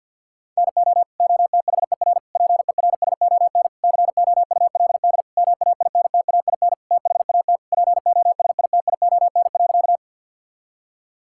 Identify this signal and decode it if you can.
Morse code, heard as a keyed tone, about 50 wpm, 690 Hz. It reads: NO OTHER PERSON COULD MAINTAIN THAT POSITION.
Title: Sentences from Top 1000 Words - Review Part 5 - 50wpm